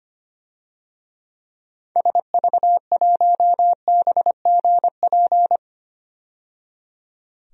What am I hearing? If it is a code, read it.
Morse code, heard as a keyed tone, about 25 wpm, 690 Hz. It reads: SV1BGP
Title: DXCC-100 Call Signs - 6 Characters - 25wpm